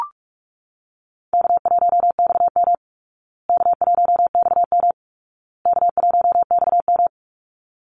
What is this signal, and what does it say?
Morse code, heard as a keyed tone, about 44 wpm, 700 Hz. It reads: K1XM K1XM K1XM
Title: Easy Call Signs Repeated 3x - 45wpm